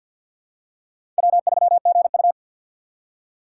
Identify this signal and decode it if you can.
Morse code, heard as a keyed tone, about 48 wpm, 700 Hz. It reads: W3GU
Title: USA Call Signs - 4 Characters - 3x Character Spacing - 50wpm